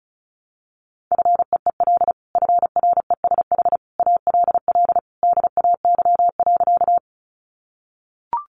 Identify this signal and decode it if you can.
Morse code, heard as a keyed tone, about 34 wpm, 700 Hz. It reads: FEEL FRESH ALL DAY.
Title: Sentences from Top 1200 Words 35wpm